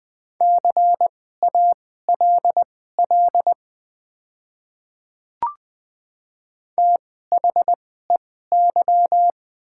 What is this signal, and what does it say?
Morse code, heard as a keyed tone, about 20 wpm, spaced as for 15 wpm, 700 Hz. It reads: CALL THEY
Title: 2x Character Spacing - Top 100 Words - 20wpm